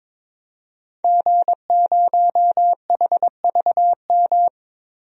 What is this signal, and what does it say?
Morse code, heard as a keyed tone, about 22 wpm, 700 Hz. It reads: G0HVM
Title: DXCC-100 Call Signs - 5 Characters - Repeated 3x - 22wpm